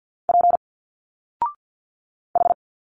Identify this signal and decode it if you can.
Morse code, heard as a keyed tone, about 49 wpm, 700 Hz. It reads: P H
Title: Single Letters 50wpm